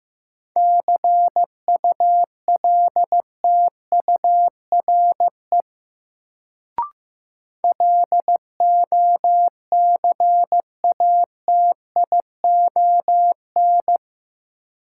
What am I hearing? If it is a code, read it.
Morse code, heard as a keyed tone, about 15 wpm, 700 Hz. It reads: CULTURE LOCATION
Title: Sentences from Top 1000 Words - Review Part 6 - New Word Drill - 15wpm